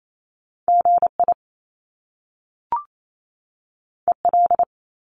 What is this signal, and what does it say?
Morse code, heard as a keyed tone, about 28 wpm, 700 Hz. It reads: GI EL
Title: Top Two Letter Combinations in Words 28wpm